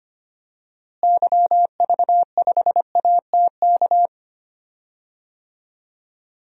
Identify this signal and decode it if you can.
Morse code, heard as a keyed tone, about 25 wpm, 700 Hz. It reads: YV5ATK
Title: DXCC-100 Call Signs - 6 Characters - Repeated 3x - 25wpm